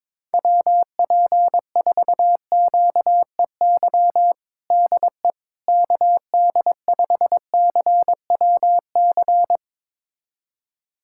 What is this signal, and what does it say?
Morse code, heard as a keyed tone, about 22 wpm, 700 Hz. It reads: WP4QEY DE KD5CWC
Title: USA Callsign DE Callsign 22wpm